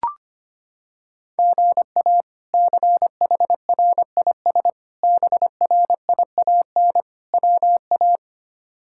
Morse code, heard as a keyed tone, about 25 wpm, 700 Hz. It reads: GA CHRIS BRIAN WA
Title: K1USN SST QSO Practice - 25wpm